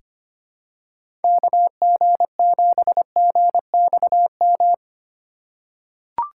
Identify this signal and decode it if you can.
Morse code, heard as a keyed tone, about 25 wpm, 700 Hz. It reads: KG7GXM
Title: USA Call Signs - 6 Characters - 4x Character Spacing - 25wpm